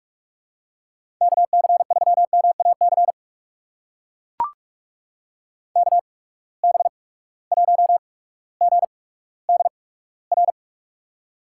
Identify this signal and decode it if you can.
Morse code, heard as a keyed tone, about 44 wpm, spaced as for 19 wpm, 700 Hz. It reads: KC3MAC KB1GDR
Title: USA Call Signs - 8x Character Spacing - 45wpm